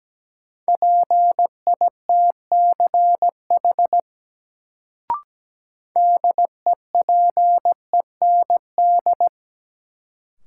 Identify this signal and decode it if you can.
Morse code, heard as a keyed tone, about 17 wpm, 700 Hz. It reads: PITCH DEPEND
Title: Sentences from Top 1300 Words - New Word Drill - 17wpm